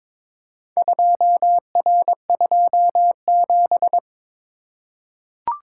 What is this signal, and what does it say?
Morse code, heard as a keyed tone, about 22 wpm, 700 Hz. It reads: 2R27
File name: practice / 22wpm / Four Letter-Number Combinations - 22wpm